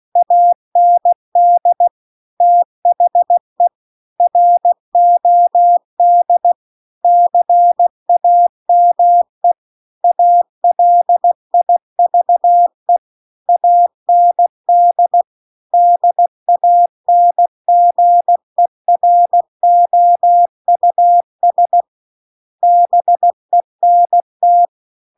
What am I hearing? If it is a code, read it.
Morse code, heard as a keyed tone, about 16 wpm, 700 Hz. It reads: AND THE ROD CAME ALIVE AND DANGEROUS BENT